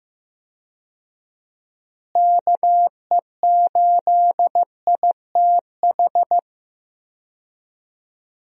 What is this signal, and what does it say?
Morse code, heard as a keyed tone, about 15 wpm, 700 Hz. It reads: KE8ITH
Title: USA Call Signs - 6 Characters - Repeated 2x - 15wpm